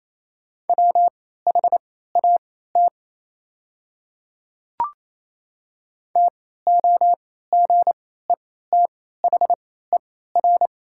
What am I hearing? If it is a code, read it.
Morse code, heard as a keyed tone, about 27 wpm, spaced as for 16 wpm, 690 Hz. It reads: WHAT TOGETHER
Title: 3x Character Spacing - Top 300 Words - 28wpm